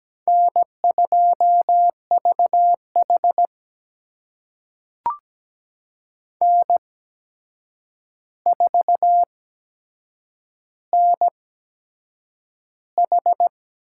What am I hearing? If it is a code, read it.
Morse code, heard as a keyed tone, about 17 wpm, spaced as for 7 wpm, 700 Hz. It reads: N2VH N4NH
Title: USA Call Signs - 4 Characters - 8x Character Spacing - 17wpm